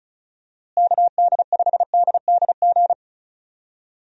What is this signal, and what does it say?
Morse code, heard as a keyed tone, about 34 wpm, 700 Hz. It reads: KD5DDG
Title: USA Call Signs - 6 Characters - Repeated 2x - 35wpm